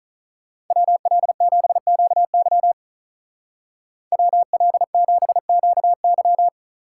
Morse code, heard as a keyed tone, about 44 wpm, 700 Hz. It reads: WL7QY WL7QY